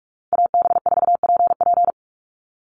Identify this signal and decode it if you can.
Morse code, heard as a keyed tone, about 44 wpm, 700 Hz. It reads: AB4PP
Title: NA SOTA Chaser Callsigns 45wpm